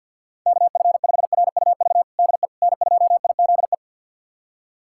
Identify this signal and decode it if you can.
Morse code, heard as a keyed tone, about 50 wpm, 700 Hz. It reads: KF5RUV DE N2IZE
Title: USA Callsign DE Callsign 50wpm